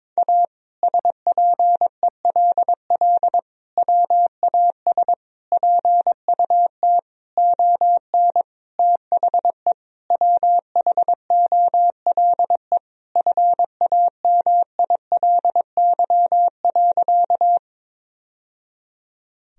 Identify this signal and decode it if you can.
Morse code, heard as a keyed tone, about 22 wpm, 700 Hz. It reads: A SPELL WAS PUT ON THE WHOLE FAMILY.
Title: Sentences from Top 1000 Words - Review Part 1 - 22wpm